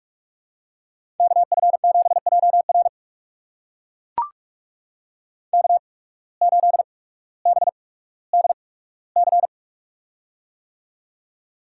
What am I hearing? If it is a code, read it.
Morse code, heard as a keyed tone, about 44 wpm, spaced as for 19 wpm, 700 Hz. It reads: KF7JR K8BDC
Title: USA Call Signs - 5 Characters - 8x Character Spacing - 45wpm